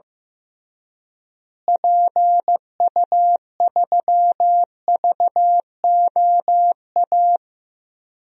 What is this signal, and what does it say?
Morse code, heard as a keyed tone, about 15 wpm, 700 Hz. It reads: PU3VOA
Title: DXCC-100 Call Signs - 6 Characters - Repeated 3x - 15wpm